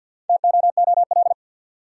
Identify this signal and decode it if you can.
Morse code, heard as a keyed tone, about 52 wpm, 700 Hz. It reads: TOOL
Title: Sentences from Top 1000 Words - Review Part 6 - New Word Drill - 50wpm